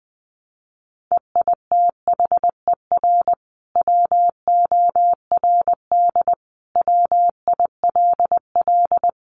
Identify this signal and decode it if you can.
Morse code, heard as a keyed tone, about 20 wpm, 700 Hz. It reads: EITHER WORD WILL
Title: Sentences from Top 600 Words 20wpm